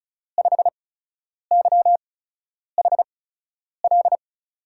Morse code, heard as a keyed tone, about 36 wpm, spaced as for 10 wpm, 700 Hz. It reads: 5YHL
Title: USA Call Signs - 6 Characters - 8x Character Spacing - 35wpm